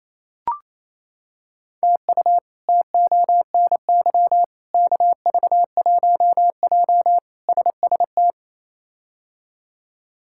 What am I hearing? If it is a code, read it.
Morse code, heard as a keyed tone, about 28 wpm, 700 Hz. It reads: TU TONY KV1J SST
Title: K1USN SST QSO Practice - 28wpm